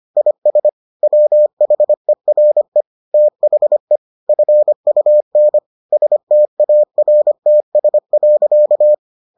Morse code, heard as a keyed tone, about 25 wpm, 600 Hz. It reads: IS WHERE THE FUN STARTS.